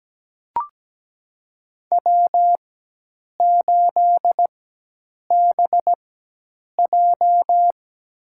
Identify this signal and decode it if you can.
Morse code, heard as a keyed tone, about 17 wpm, spaced as for 8 wpm, 700 Hz. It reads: W8BJ
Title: USA Call Signs - 4 Characters - 4x Character Spacing - 17wpm